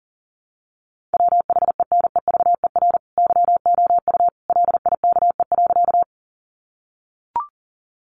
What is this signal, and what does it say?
Morse code, heard as a keyed tone, about 40 wpm, 700 Hz. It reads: WHENEVER YOU LIKE.
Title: Sentences from Top 1200 Words 40wpm